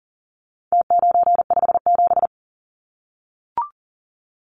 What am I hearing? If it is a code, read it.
Morse code, heard as a keyed tone, about 40 wpm, 700 Hz. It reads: T957
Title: Numbers - Contest Style - 4-Digit 40wpm